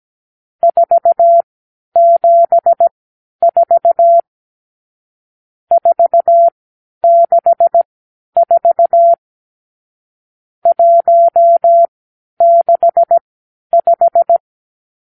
Morse code, heard as a keyed tone, about 17 wpm, spaced as for 10 wpm, 680 Hz. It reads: 474 464 165
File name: k7qo / Lesson 52 (8)